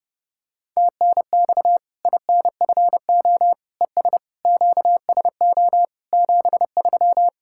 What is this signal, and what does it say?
Morse code, heard as a keyed tone, about 30 wpm, 700 Hz. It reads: TNX INFO ES QSO 73
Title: K1USN SST QSO Practice 30wpm